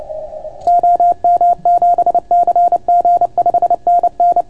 Morse code, heard as a keyed tone, about 28 wpm, 700 Hz. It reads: OM7CG 5NN